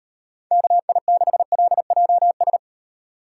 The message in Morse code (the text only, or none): KI6LJS